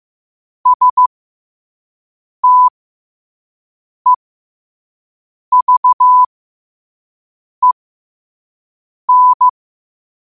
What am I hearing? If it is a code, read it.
Morse code, heard as a keyed tone, about 15 wpm, spaced as for 5 wpm, 1000 Hz. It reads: STEVEN